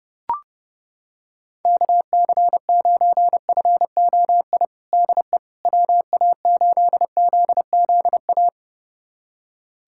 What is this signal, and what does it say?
Morse code, heard as a keyed tone, about 32 wpm, 700 Hz. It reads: KC9FOI DE WA8ZZA
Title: USA Callsign DE Callsign - 6 Characters - 30wpm